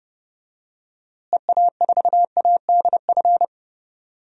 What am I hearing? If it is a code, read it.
Morse code, heard as a keyed tone, about 30 wpm, 700 Hz. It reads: EA4ADF
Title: DXCC-100 Call Signs - 6 Characters - Repeated 2x - 30wpm